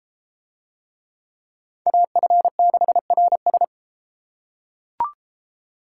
Morse code, heard as a keyed tone, about 31 wpm, 700 Hz. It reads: AF6RS